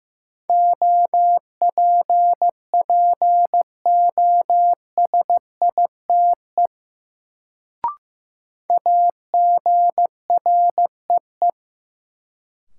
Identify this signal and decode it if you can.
Morse code, heard as a keyed tone, about 15 wpm, 700 Hz. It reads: OPPOSITE AGREE
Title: Sets of 5 Words Encore - New Word Drill - 15wpm